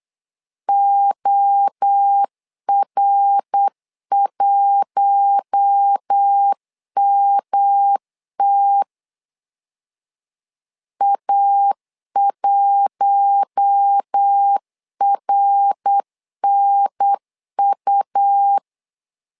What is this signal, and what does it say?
Morse code, heard as a keyed tone, about 8 wpm, 800 Hz. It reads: OR1MT A1RNU
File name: LEZIONI / 4 Q1 8wpm 3035